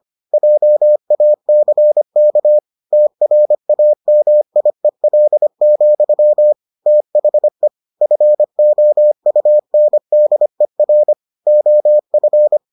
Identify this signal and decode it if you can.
Morse code, heard as a keyed tone, about 25 wpm, 600 Hz. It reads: JACK TRAMIEL, THE FOUNDER OF